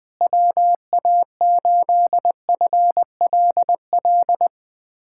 Morse code, heard as a keyed tone, about 20 wpm, 700 Hz. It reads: WA8FLL
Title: USA Callsign DE Callsign - 6 Characters - 20wpm